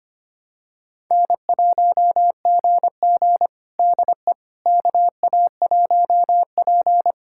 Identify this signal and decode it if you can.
Morse code, heard as a keyed tone, about 25 wpm, 700 Hz. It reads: N1GG DE KA1P
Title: USA Callsign DE Callsign - 4 Characters - 25wpm